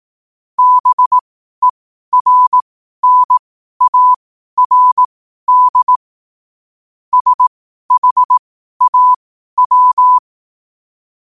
Morse code, heard as a keyed tone, about 18 wpm, spaced as for 12 wpm, 1000 Hz. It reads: BERNARD SHAW
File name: qotdfiles / 12wpm